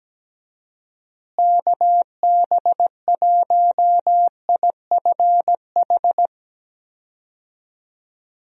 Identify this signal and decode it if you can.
Morse code, heard as a keyed tone, about 17 wpm, 700 Hz. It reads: KB1IFH
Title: USA Call Signs - 6 Characters - Repeated 3x - 17wpm